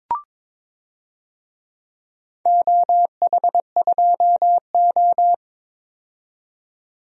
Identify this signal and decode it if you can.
Morse code, heard as a keyed tone, about 22 wpm, 700 Hz. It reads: OH2O
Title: DXCC-100 Call Signs - 4 Characters - 22wpm